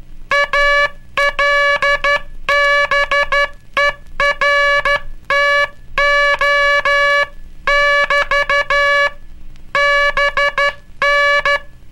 Morse code, heard as a keyed tone, about 12 wpm, 1700 Hz, with noise. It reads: ALBERTO <BT> BN
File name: cursos / Curso CW mp3 / CW Aula - 18